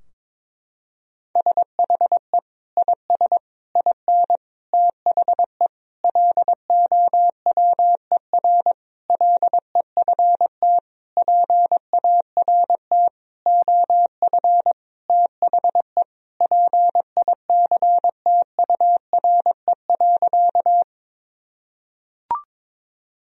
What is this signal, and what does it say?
Morse code, heard as a keyed tone, about 22 wpm, 700 Hz. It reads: SHE IS IN THE LOWER LEFT PART OF THE PICTURE.
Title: Sentences from Top 1000 Words - Review Part 4 - 22wpm